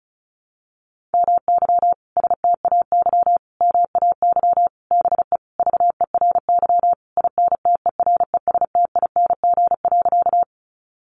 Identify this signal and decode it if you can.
Morse code, heard as a keyed tone, about 34 wpm, 700 Hz. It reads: MY STAY MAY BE VERY INTERESTING.
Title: Sentences from Top 1000 Words 35wpm